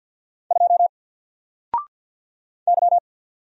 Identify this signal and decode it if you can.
Morse code, heard as a keyed tone, about 49 wpm, 700 Hz. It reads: 2 Y
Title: Single Character (Letter:Number:Punctuation:Prosign) 50wpm